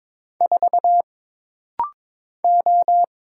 Morse code, heard as a keyed tone, about 22 wpm, 700 Hz. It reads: 4 O
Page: This rapid-fire timing is geared to push your ICR (Instant Character Recognition) skills.